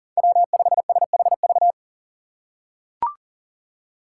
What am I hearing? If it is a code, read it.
Morse code, heard as a keyed tone, about 40 wpm, 700 Hz. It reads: W5SHV
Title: USA Call Signs - 5 Characters - 2x Character Spacing - 40wpm